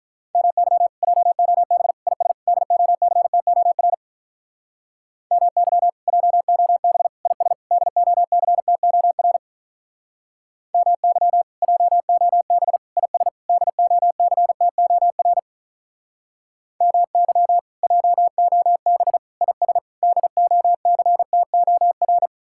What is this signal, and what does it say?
Morse code, heard as a keyed tone, about 46 wpm, 700 Hz. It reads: MY JOB IS DOCTOR MY JOB IS DOCTOR MY JOB IS DOCTOR MY JOB IS DOCTOR